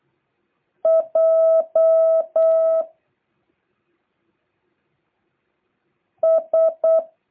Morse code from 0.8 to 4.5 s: J